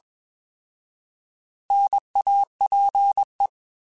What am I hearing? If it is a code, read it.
Morse code, heard as a keyed tone, about 21 wpm, 800 Hz. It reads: NAPE
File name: Alphabetical / Alphabetical mp3s / Ns @20